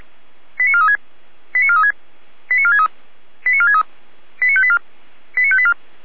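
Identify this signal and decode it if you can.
Five-tone selective call (ZVEI), heard as quick runs of five tones, tones of about 70 ms. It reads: 89436 89436 89463 89463 89584 89584
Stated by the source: ZVEI five-tone call 89416 received on an SDR, from the feuerwehr-orlamuende.org fire brigade audio archive.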